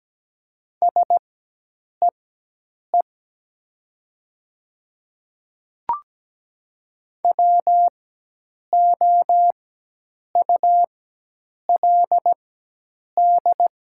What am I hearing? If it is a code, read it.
Morse code, heard as a keyed tone, about 17 wpm, spaced as for 8 wpm, 700 Hz. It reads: SEE WOULD
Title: 4x Character Spacing - Top 200 Words - 17wpm